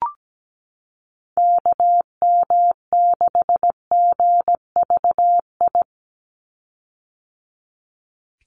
Morse code, heard as a keyed tone, about 17 wpm, 700 Hz. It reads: KM6GVI
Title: USA Call Signs 17wpm